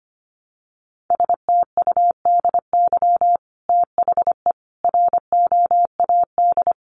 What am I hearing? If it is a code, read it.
Morse code, heard as a keyed tone, about 25 wpm, 700 Hz. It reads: STUDY THE ROAD
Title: Sentences from Top 1300 Words 25wpm